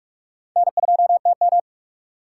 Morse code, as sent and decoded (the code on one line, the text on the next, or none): -. ..--- - --
N2TM